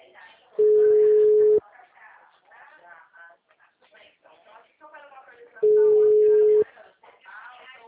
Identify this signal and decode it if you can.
Ringback tone (Europe, 425 Hz), with speech.